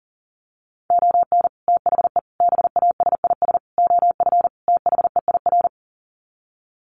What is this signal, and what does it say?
Morse code, heard as a keyed tone, about 40 wpm, 700 Hz. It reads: ON THE BASIS OF THEIR
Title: Sets of 5 Words Encore 40wpm